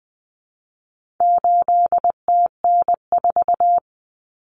Morse code, heard as a keyed tone, about 20 wpm, 700 Hz. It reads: 8TN4
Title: Common Cut Numbers - 4-Digit 20wpm